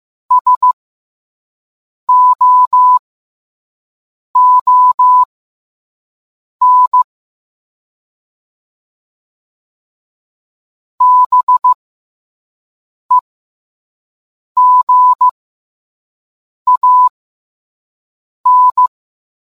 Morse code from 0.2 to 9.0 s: SOON